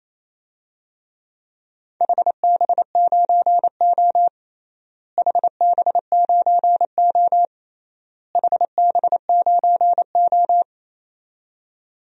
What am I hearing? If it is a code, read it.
Morse code, heard as a keyed tone, about 28 wpm, 700 Hz. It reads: HB9O HB9O HB9O
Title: DXCC-100 Call Signs - 4 Characters - Repeated 3x - 28wpm